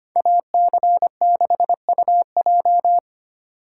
Morse code, heard as a keyed tone, about 30 wpm, 700 Hz. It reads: AC6UJ
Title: Speed Racing - USA 2x2 Call Signs 25wpm